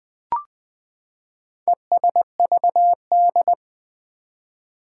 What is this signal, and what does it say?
Morse code, heard as a keyed tone, about 20 wpm, 700 Hz. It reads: ESVD
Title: Four Challenging Letter-Number Combinations 20wpm